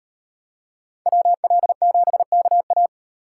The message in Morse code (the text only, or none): WL7KA